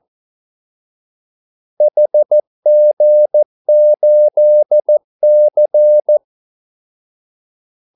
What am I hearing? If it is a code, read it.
Morse code, heard as a keyed tone, about 14 wpm, 600 Hz. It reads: HG8C
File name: Callsigns-14wpm